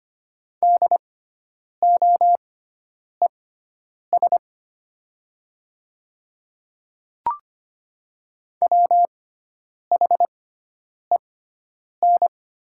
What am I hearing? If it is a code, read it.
Morse code, heard as a keyed tone, about 25 wpm, spaced as for 9 wpm, 700 Hz. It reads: DOES WHEN
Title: 6x Character Spacing - Top 200 Words - 25wpm